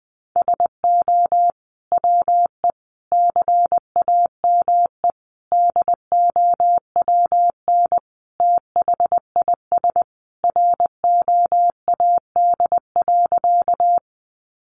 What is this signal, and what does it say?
Morse code, heard as a keyed tone, about 20 wpm, 700 Hz. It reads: SO WE CAME DOWN THIS ROAD.
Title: Sentences from Top 400 Words 20wpm